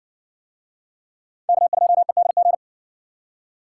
Morse code, heard as a keyed tone, about 56 wpm, 700 Hz. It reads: K2EDG